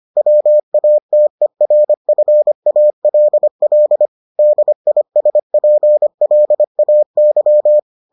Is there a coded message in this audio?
Yes: Morse code from 0.2 to 7.8 s, heard as a keyed tone, about 25 wpm, 600 Hz. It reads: WATERFALL DISPLAY